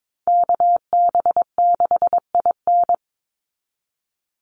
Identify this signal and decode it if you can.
Morse code, heard as a keyed tone, about 22 wpm, 700 Hz. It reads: KB6IN
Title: USA Call Signs 22wpm